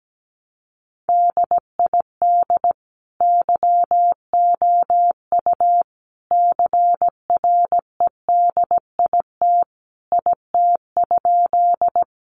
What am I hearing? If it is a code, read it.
Morse code, heard as a keyed tone, about 17 wpm, 700 Hz. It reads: DID YOU CREDIT IT?
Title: Sentences from Top 1000 Words 17wpm